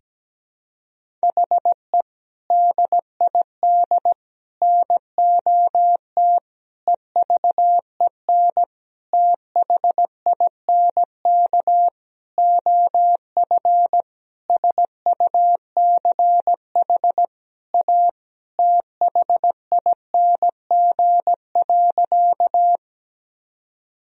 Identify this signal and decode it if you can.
Morse code, heard as a keyed tone, about 17 wpm, 700 Hz. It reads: HE DID NOT EVEN THINK OF SUCH A THING.
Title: Sentences from Top 1000 Words - Review Part 1 - 17wpm